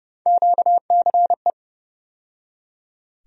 Morse code, heard as a keyed tone, about 30 wpm, 700 Hz. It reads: QCE
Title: USA Call Sign Postfixes - 3 Letters 30wpm